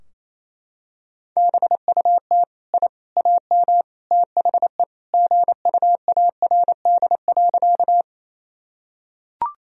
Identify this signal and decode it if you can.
Morse code, heard as a keyed tone, about 28 wpm, 700 Hz. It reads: BUT I AM THE GUARD.
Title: Sentences from Top 1000 Words - Review Part 4 - 28wpm